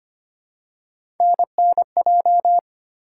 Morse code, heard as a keyed tone, about 25 wpm, 700 Hz. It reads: NNJ
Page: morse-code-ninja-practice-25wpm